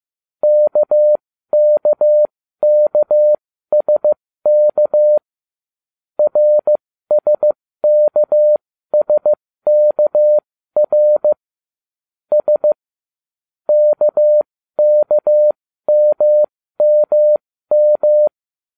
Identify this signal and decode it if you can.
Morse code, heard as a keyed tone, about 15 wpm, spaced as for 13 wpm, 600 Hz. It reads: KKKSK RSKSKR S KKMMM